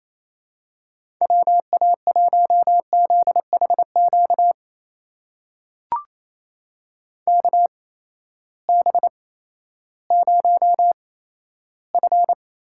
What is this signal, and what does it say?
Morse code, heard as a keyed tone, about 28 wpm, spaced as for 14 wpm, 700 Hz. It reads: WA1ZHQ KB0F
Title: USA Call Signs - 6 Characters - 8x Character Spacing - 28wpm